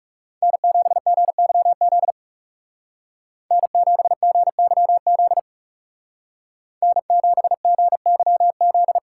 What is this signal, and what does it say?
Morse code, heard as a keyed tone, about 39 wpm, 700 Hz. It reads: N7GYZ N7GYZ N7GYZ